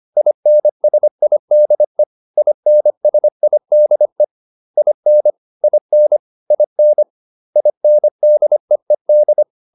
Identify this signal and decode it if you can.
Morse code, heard as a keyed tone, about 25 wpm, 600 Hz. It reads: INSIDE INSIDE IN IN IN INDEED